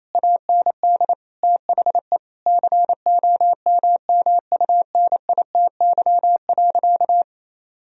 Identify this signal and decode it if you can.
Morse code, heard as a keyed tone, about 28 wpm, 700 Hz. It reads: AND THE COMMUNITY.
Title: Sentences from Top 1000 Words - Review Part 4 - 28wpm